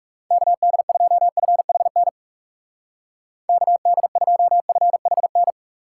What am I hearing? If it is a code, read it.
Morse code, heard as a keyed tone, about 40 wpm, 700 Hz. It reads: KD2FHN KD2FHN